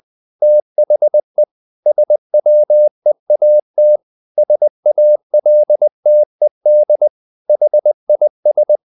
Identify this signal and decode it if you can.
Morse code, heard as a keyed tone, about 20 wpm, 600 Hz. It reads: THE SWEAT SALTED HIS
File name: the old man and the sea part 2 20wpm